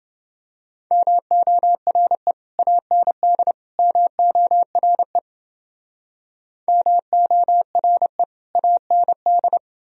Morse code, heard as a keyed tone, about 30 wpm, 700 Hz. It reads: MORE AND MORE MORE AND